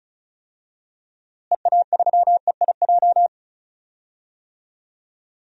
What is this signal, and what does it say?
Morse code, heard as a keyed tone, about 36 wpm, 700 Hz. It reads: EA3EIJ
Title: DXCC-100 Call Signs - 6 Characters - Repeated 3x - 35wpm